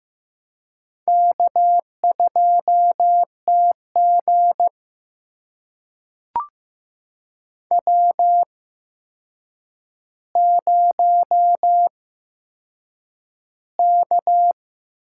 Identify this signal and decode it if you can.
Morse code, heard as a keyed tone, about 15 wpm, spaced as for 7 wpm, 700 Hz. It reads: K2TG W0K
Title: USA Call Signs - 8x Character Spacing - 15wpm